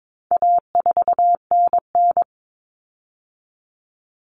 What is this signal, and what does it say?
Morse code, heard as a keyed tone, about 22 wpm, 700 Hz. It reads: A4NN
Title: Common Cut Numbers - 4-Digit 22wpm